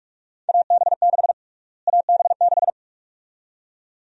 Morse code, heard as a keyed tone, about 46 wpm, 700 Hz. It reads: AB6 AB6
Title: USA Call Sign Prefixes - Repeated 2x - 45wpm